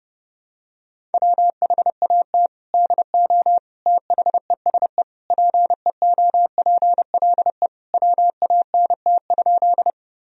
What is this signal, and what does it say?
Morse code, heard as a keyed tone, about 32 wpm, 700 Hz. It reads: WHAT DO THESE PEOPLE WANT?
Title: Sentences from Top 1000 Words - Review Part 1 - 30wpm